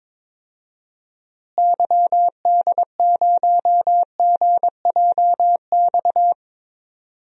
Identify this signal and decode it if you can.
Morse code, heard as a keyed tone, about 22 wpm, 700 Hz. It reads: YD0GJX
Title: DXCC-100 Call Signs - 6 Characters - Repeated 2x - 22wpm